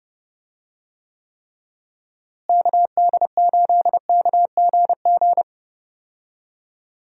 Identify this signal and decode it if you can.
Morse code, heard as a keyed tone, about 30 wpm, 700 Hz. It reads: KD8KGG
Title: USA Call Signs - 6 Characters - Repeated 3x - 30wpm